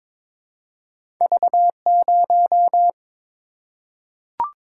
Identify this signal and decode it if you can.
Morse code, heard as a keyed tone, about 22 wpm, 700 Hz. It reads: V0
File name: practice / 22wpm / Two Letter-Number Combinations - 22wpm